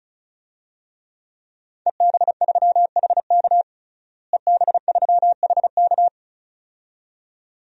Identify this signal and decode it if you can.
Morse code, heard as a keyed tone, about 36 wpm, 700 Hz. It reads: EB3HK EB3HK
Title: DXCC-100 Call Signs - 5 Characters - Repeated 2x - 35wpm